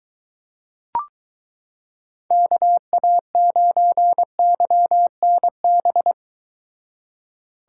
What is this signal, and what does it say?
Morse code, heard as a keyed tone, about 22 wpm, 700 Hz. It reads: KA9YNB